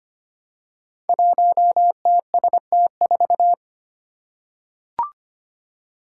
Morse code, heard as a keyed tone, about 25 wpm, 700 Hz. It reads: 1TST4
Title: Five Challenging Letter-Number Combinations 25wpm